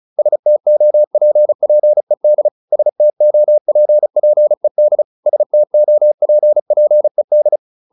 Morse code, heard as a keyed tone, about 34 wpm, 600 Hz. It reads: STOPPED STOPPED STOPPED